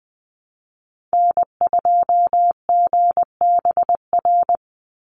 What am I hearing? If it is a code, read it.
Morse code, heard as a keyed tone, about 20 wpm, 700 Hz. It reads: N2GBR
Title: NA SOTA Activator Callsigns 20wpm